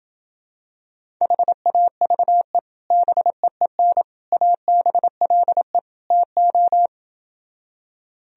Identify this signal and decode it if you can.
Morse code, heard as a keyed tone, about 28 wpm, 700 Hz. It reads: HAVE BEEN ABLE TO